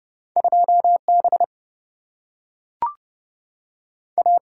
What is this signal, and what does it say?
Morse code, heard as a keyed tone, about 31 wpm, 700 Hz. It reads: 2 B A
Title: Single-Letter Single-Number 30wpm